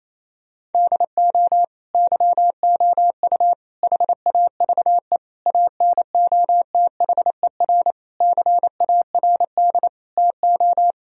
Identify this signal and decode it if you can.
Morse code, heard as a keyed tone, about 28 wpm, 700 Hz. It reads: DO YOU HAVE ANOTHER CARD TO
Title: Sentences from Top 1000 Words - Review Part 4 - 28wpm